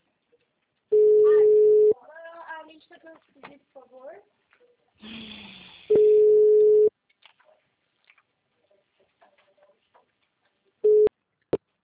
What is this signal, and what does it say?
Ringback tone (Europe, 425 Hz), with speech.